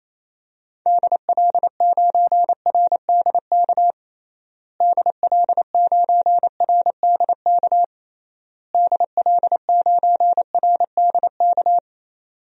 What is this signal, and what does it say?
Morse code, heard as a keyed tone, about 28 wpm, 700 Hz. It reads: DL9RDK DL9RDK DL9RDK
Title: DXCC-100 Call Signs - 6 Characters - Repeated 3x - 28wpm